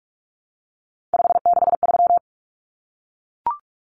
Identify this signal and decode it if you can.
Morse code, heard as a keyed tone, about 44 wpm, 700 Hz. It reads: 563
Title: Numbers - 3-Digit 45wpm